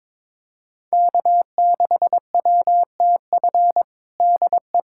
Morse code, heard as a keyed tone, about 21 wpm, 700 Hz. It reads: K6WTF DE
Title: USA Callsign DE Callsign - 5 Characters - 22wpm